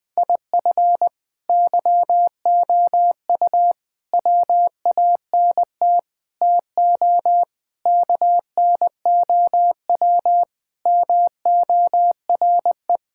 Morse code, heard as a keyed tone, about 20 wpm, 700 Hz. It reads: IF YOU WANT TO KNOW MORE
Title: Sentences with 8-Word Phrases - 20wpm